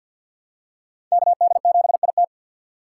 Morse code, heard as a keyed tone, about 52 wpm, 700 Hz. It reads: KD7IT
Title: USA Call Signs - 5 Characters - 3x Character Spacing - 50wpm